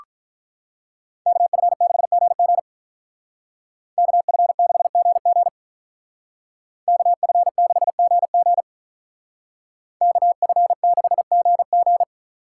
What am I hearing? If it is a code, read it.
Morse code, heard as a keyed tone, about 44 wpm, 700 Hz. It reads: KF6GG KF6GG KF6GG KF6GG
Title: Speed Racing - USA Call Signs 35wpm